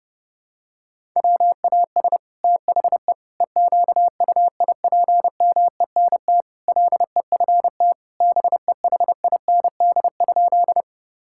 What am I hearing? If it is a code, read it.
Morse code, heard as a keyed tone, about 30 wpm, 700 Hz. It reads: WAS THE EQUIPMENT LEFT BEHIND?
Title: Sentences from Top 1000 Words - Review Part 5 - 30wpm